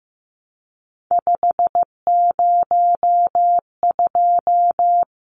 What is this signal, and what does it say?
Morse code, heard as a keyed tone, about 15 wpm, 700 Hz. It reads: 502
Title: Numbers - 3-Digit 15wpm